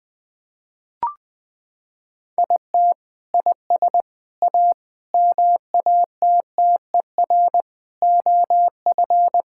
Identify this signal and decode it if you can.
Morse code, heard as a keyed tone, about 20 wpm, 700 Hz. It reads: IT IS A MATTER OF
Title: Sentences from Top 1000 Words - Review Part 5 - 20wpm